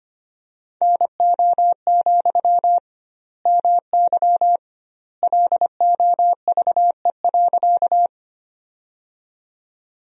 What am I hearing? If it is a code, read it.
Morse code, heard as a keyed tone, about 25 wpm, 700 Hz. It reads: NO, MY LOVE.
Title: 2x Word-Spacing - Sentences from Top 400 Words - 25wpm